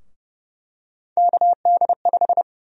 Morse code, heard as a keyed tone, about 31 wpm, 700 Hz. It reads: KD5
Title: USA Call Sign Prefixes - Repeated 2x - 30wpm